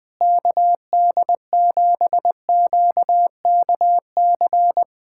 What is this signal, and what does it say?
Morse code, heard as a keyed tone, about 20 wpm, 700 Hz. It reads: KD7QKC
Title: USA Callsign DE Callsign 20wpm